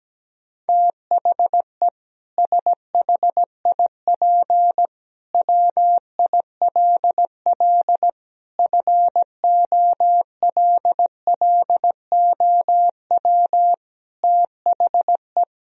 Morse code, heard as a keyed tone, about 17 wpm, 700 Hz. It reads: THE SHIP WILL FOLLOW THE
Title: Sentences from Top 1000 Words - Review Part 4 - 17wpm